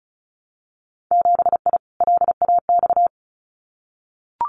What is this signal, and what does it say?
Morse code, heard as a keyed tone, about 36 wpm, 700 Hz. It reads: 7I LAX
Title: Winter Field Day Exchanges Encore 35wpm